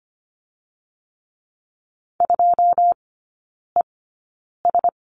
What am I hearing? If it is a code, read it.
Morse code, heard as a keyed tone, about 25 wpm, spaced as for 9 wpm, 690 Hz. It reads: 2ES
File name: Lesson 14e - Number 2 - Three Characters 25wpm